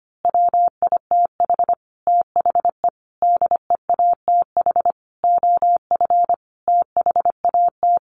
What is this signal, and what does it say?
Morse code, heard as a keyed tone, about 25 wpm, 700 Hz. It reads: WITH THE DEATH OF THAT
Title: A Sound Of Thunder 25wpm